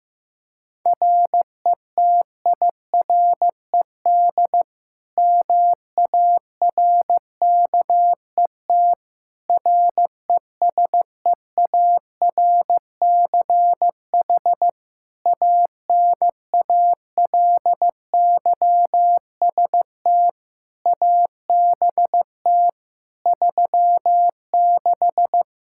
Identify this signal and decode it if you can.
Morse code, heard as a keyed tone, about 15 wpm, 700 Hz. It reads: RETIRED MARKET RESEARCH ANALYST ABT 36
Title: Practice QSO - Exchange 3 - More Info - 15wpm